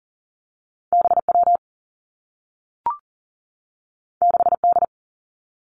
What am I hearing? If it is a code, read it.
Morse code, heard as a keyed tone, about 42 wpm, 700 Hz. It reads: B W 6 D
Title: Two Challenging Letter-Number Combinations 40wpm